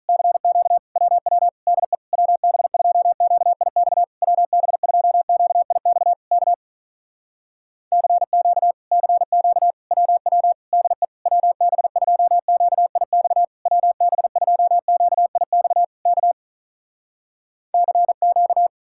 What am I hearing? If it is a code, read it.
Morse code, heard as a keyed tone, about 48 wpm, 700 Hz. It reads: CQ WW DE WB2QIX WB2QIX K CQ CQ WW DE WB2QIX WB2QIX K CQ